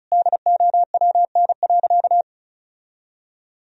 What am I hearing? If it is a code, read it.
Morse code, heard as a keyed tone, about 36 wpm, 700 Hz. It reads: DOWN.
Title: Sentences from Top 1000 Words - Review Part 2 - 35wpm